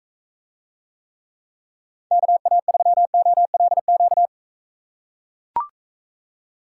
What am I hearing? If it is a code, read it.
Morse code, heard as a keyed tone, about 43 wpm, 700 Hz. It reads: KA3OLQ